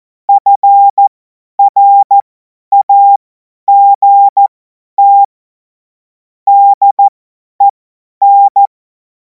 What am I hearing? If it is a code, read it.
Morse code, heard as a keyed tone, about 14 wpm, spaced as for 10 wpm, 800 Hz. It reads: FRAGT DEN
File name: Morsekurs / Witze / Witz 17. Februar 2026 50 F70